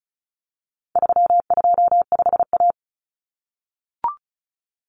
Morse code, heard as a keyed tone, about 34 wpm, 700 Hz. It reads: 325A
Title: Common Cut Numbers - 4-Digit 35wpm